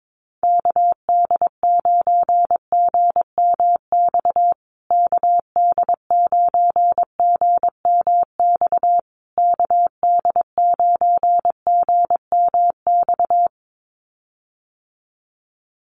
Morse code, heard as a keyed tone, about 22 wpm, 700 Hz. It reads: KD9GMX KD9GMX KD9GMX
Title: USA Call Signs Repeated 3x 22wpm